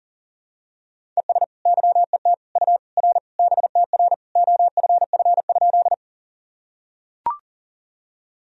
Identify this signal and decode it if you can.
Morse code, heard as a keyed tone, about 40 wpm, 700 Hz. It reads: ES YET U R BTR OFF?
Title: Sentences from Top 500 Words and Abbreviations 40wpm